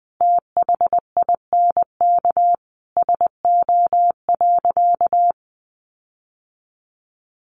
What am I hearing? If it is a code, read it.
Morse code, heard as a keyed tone, about 20 wpm, 700 Hz. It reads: THINK SO.
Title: Sentences from Top 1500 Words 20wpm